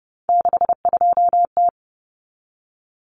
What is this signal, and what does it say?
Morse code, heard as a keyed tone, about 30 wpm, 700 Hz. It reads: 62T
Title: Common Cut Numbers - 3-Digit 30wpm